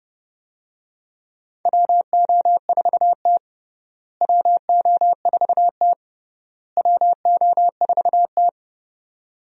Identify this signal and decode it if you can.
Morse code, heard as a keyed tone, about 30 wpm, 700 Hz. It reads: WO4T WO4T WO4T
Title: USA Call Signs - 4 Characters - Repeated 3x - 30wpm